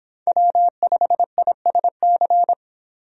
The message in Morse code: W5ISC

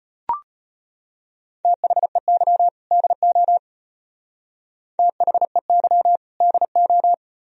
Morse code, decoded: THEY DO THEY DO

